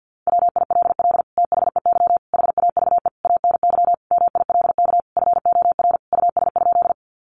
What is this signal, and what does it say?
Morse code, heard as a keyed tone, about 48 wpm, 700 Hz. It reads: WILL THEY HAVE ANY MILK FOR US?
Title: Sentences from Top 1500 Words 50wpm